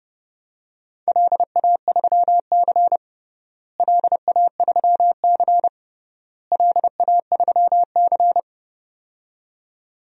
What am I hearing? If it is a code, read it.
Morse code, heard as a keyed tone, about 31 wpm, 700 Hz. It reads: LA3C LA3C LA3C
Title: DXCC-100 Call Signs - 4 Characters - Repeated 3x - 30wpm